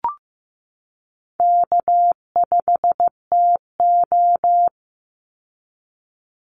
Morse code, heard as a keyed tone, about 15 wpm, 700 Hz. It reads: K5TO
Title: Easy Call Signs - 15wpm